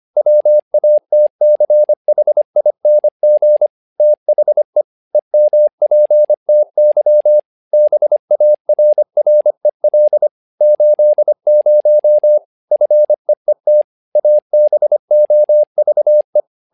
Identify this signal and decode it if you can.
Morse code, heard as a keyed tone, about 25 wpm, 600 Hz. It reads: WATCHING THE EMPTY BARREL 80 FEET ABOVE
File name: learning center / tower trouble / tower trouble 25wpm